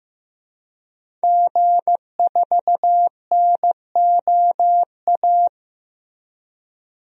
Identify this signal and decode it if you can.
Morse code, heard as a keyed tone, about 16 wpm, 700 Hz. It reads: G4NOA
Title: DXCC-100 Call Signs - 5 Characters - Repeated 3x - 15wpm